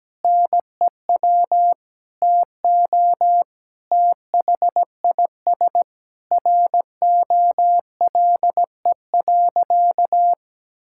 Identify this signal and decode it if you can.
Morse code, heard as a keyed tone, about 17 wpm, 700 Hz. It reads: NEW TO THIS ROLE.
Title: Sentences from Top 1000 Words - Review Part 6 - 17wpm